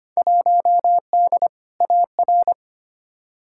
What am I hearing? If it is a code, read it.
Morse code, heard as a keyed tone, about 25 wpm, 700 Hz. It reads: 1D AR
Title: Field Day Exchanges Encore - 25wpm